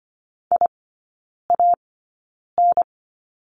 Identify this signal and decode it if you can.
Morse code, heard as a keyed tone, about 25 wpm, spaced as for 9 wpm, 700 Hz. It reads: IAN
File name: course / 25:10wpm / 093 Lesson 16e - Letter c - Three Characters 25wpm